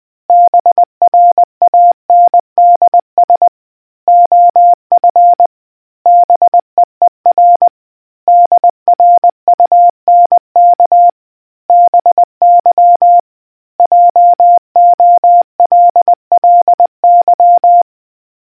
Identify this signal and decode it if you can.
Morse code, heard as a keyed tone, about 20 wpm, 700 Hz. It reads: BRANDS OF BEER DRUNK BY JOLLY